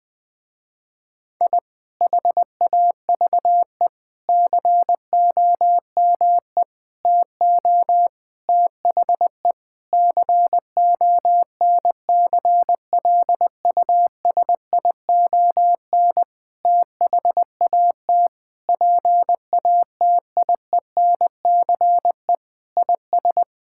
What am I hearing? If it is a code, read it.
Morse code, heard as a keyed tone, about 20 wpm, 700 Hz. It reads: I HAVE COME TO THE CONCLUSION THAT PATIENCE IS
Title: Sentences with 7-Word Phrases - Spoken Beginning - 20wpm